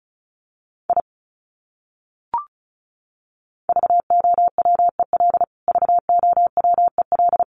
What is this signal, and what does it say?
Morse code, heard as a keyed tone, about 34 wpm, 700 Hz. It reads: I VOWEL VOWEL
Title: Top 500 Words Repeated 35wpm